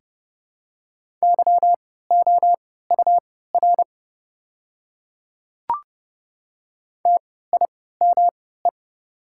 Morse code, heard as a keyed tone, about 30 wpm, spaced as for 17 wpm, 700 Hz. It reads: YOUR TIME
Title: 3x Character Spacing - Top 300 Words - 30wpm